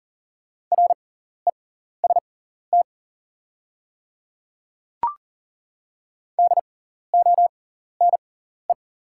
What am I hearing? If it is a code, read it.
Morse code, heard as a keyed tone, about 37 wpm, spaced as for 14 wpm, 700 Hz. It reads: REST DONE
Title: 6x Character Spacing - Top 500 Words - 40wpm